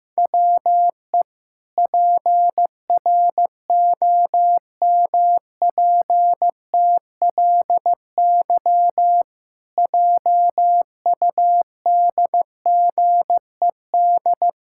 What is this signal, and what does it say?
Morse code, heard as a keyed tone, about 15 wpm, 700 Hz. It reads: WE PROMPTLY JUDGED
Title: Pangram Challenge v2 - 15wpm